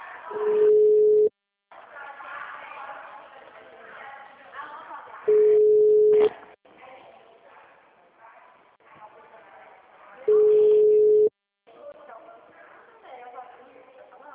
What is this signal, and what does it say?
Ringback tone (Europe, 425 Hz), with speech.
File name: out- -2011- - - .29314